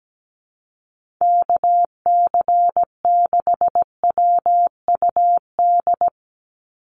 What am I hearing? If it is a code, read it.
Morse code, heard as a keyed tone, about 17 wpm, 700 Hz. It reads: KC6WUD
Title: USA Call Signs 17wpm